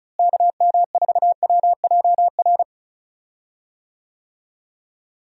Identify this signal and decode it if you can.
Morse code, heard as a keyed tone, about 36 wpm, 700 Hz. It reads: KM4WJR
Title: USA Call Signs - 6 Characters - Repeated 2x - 35wpm